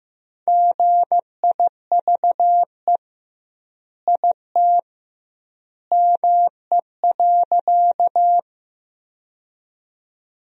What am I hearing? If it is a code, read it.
Morse code, heard as a keyed tone, about 15 wpm, 700 Hz. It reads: GIVE IT ME.
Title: 2x Word-Spacing - Sentences from Top 200 Words - 15wpm